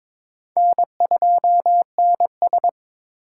Morse code, heard as a keyed tone, about 21 wpm, 700 Hz. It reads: N2NS
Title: K1USN SST QSO Practice - 22wpm